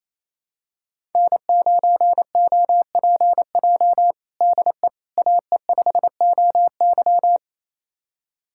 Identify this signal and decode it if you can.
Morse code, heard as a keyed tone, about 28 wpm, 700 Hz. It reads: N9OPJ DE AE5OY
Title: USA Callsign DE Callsign - 5 Characters - 28wpm